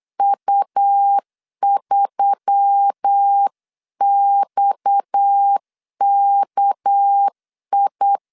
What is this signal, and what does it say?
Morse code, heard as a keyed tone, about 8 wpm, 800 Hz. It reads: U3XKI
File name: 8wpm